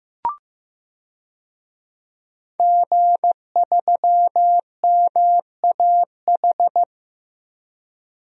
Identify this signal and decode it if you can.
Morse code, heard as a keyed tone, about 16 wpm, 700 Hz. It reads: G3MAH
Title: DXCC-100 Call Signs - 5 Characters - Repeated 2x - 15wpm